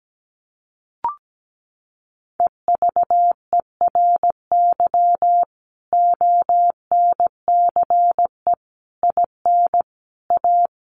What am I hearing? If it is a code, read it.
Morse code, heard as a keyed tone, about 17 wpm, 700 Hz. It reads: EVERY ONCE IN A
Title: Sets of 5 Words 17wpm